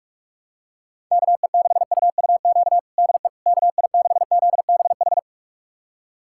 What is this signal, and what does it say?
Morse code, heard as a keyed tone, about 47 wpm, 700 Hz. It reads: KE6UUQ DE KI6ZBH
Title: USA Callsign DE Callsign 45wpm